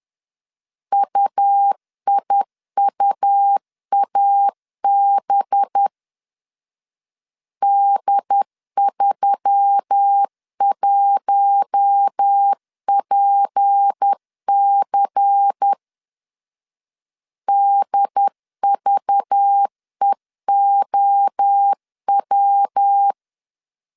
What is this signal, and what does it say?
Morse code, heard as a keyed tone, about 10 wpm, 800 Hz. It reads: UIUAB D31PC DVEOW
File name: CW / DET / 1 6 1500C S B 10wpm 4137